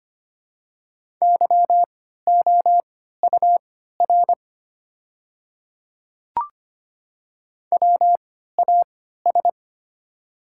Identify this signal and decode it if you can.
Morse code, heard as a keyed tone, about 25 wpm, spaced as for 14 wpm, 700 Hz. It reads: YOUR WAS
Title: 3x Character Spacing - Top 300 Words - 25wpm